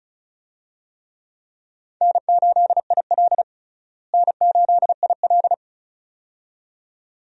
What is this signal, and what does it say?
Morse code, heard as a keyed tone, about 34 wpm, 700 Hz. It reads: N8IL N8IL
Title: USA Call Signs - 4 Characters - Repeated 2x - 35wpm